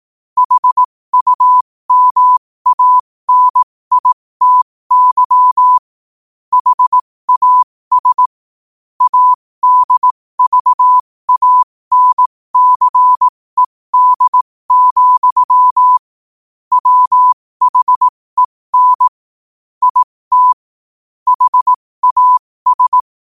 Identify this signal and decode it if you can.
Morse code, heard as a keyed tone, about 18 wpm, spaced as for 15 wpm, 1000 Hz. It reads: HUMANITY HAS ADVANCED, WHEN IT HAS
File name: qotdfiles / 15wpm